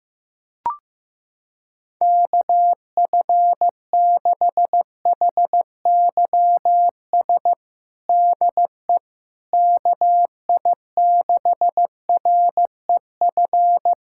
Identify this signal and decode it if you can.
Morse code, heard as a keyed tone, about 16 wpm, 700 Hz. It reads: KF6HYS DE KI6REF
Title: USA Callsign DE Callsign - 6 Characters - 15wpm